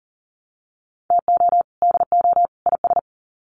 Morse code, heard as a keyed tone, about 40 wpm, 700 Hz. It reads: TO DO IS
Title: Sets of 3 Words 40wpm